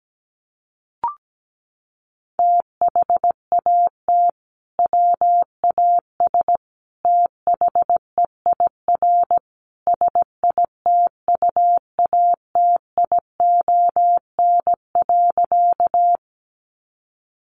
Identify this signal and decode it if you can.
Morse code, heard as a keyed tone, about 17 wpm, 700 Hz. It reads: THAT WAS THEIR SITUATION.
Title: Sentences from Top 1200 Words 17wpm